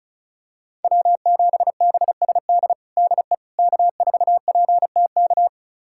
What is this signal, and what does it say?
Morse code, heard as a keyed tone, about 34 wpm, 700 Hz. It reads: W7BSD DE K4PTK
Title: USA Callsign DE Callsign - 5 Characters - 35wpm